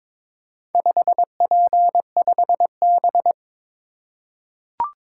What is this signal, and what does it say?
Morse code, heard as a keyed tone, about 22 wpm, 700 Hz. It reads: 5P5B
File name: practice / 22wpm / Four Letter-Number Combinations - 22wpm